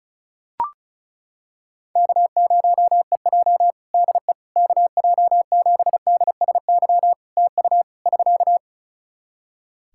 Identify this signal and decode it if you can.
Morse code, heard as a keyed tone, about 36 wpm, 700 Hz. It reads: K0EJ DE KJ7DSY TU <SK>
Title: Practice QSO - Exchange 4 - Ending - 35wpm